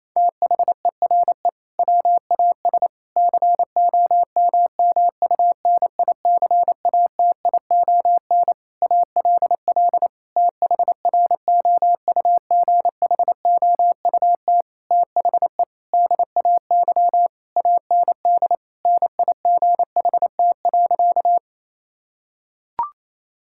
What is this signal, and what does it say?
Morse code, heard as a keyed tone, about 28 wpm, 700 Hz. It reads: THERE WAS COMMUNICATION ALL THROUGHOUT THE DAY AND NIGHT.
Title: Sentences from Top 1000 Words - Review Part 6 - 28wpm